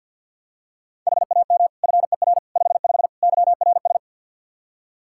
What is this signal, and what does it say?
Morse code, heard as a keyed tone, about 46 wpm, 700 Hz. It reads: HAM FER 55 YRS
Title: Practice QSO - Exchange 3 - More Info - 50wpm